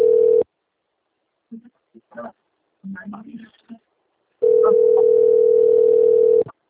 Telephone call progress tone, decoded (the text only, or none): ringback tone